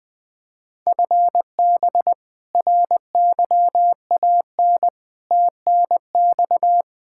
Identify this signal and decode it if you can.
Morse code, heard as a keyed tone, about 20 wpm, 700 Hz. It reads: FB RYAN TNX
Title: K1USN SST QSO Practice 20wpm